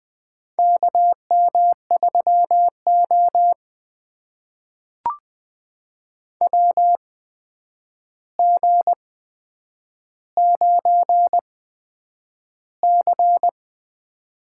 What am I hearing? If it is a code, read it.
Morse code, heard as a keyed tone, about 20 wpm, spaced as for 9 wpm, 700 Hz. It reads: KM3O WG9C
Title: USA Call Signs - 4 Characters - 8x Character Spacing - 20wpm